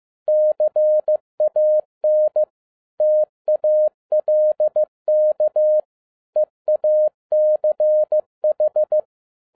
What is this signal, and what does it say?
Morse code, heard as a keyed tone, about 15 wpm, 610 Hz. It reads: CAN TALK EACH